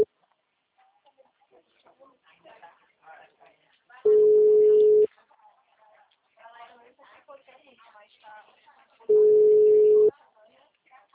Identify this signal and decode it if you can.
Ringback tone (Europe, 425 Hz), with speech.
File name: out- -2009- - - .66943